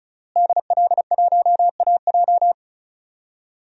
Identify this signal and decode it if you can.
Morse code, heard as a keyed tone, about 34 wpm, 700 Hz. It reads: DL1AJ
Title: DXCC-100 Call Signs - 5 Characters - Repeated 3x - 35wpm